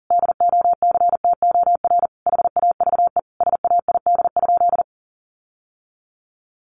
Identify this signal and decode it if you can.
Morse code, heard as a keyed tone, about 40 wpm, 700 Hz. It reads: DOCTOR HAVE SAID?
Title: Sentences from Top 1500 Words 40wpm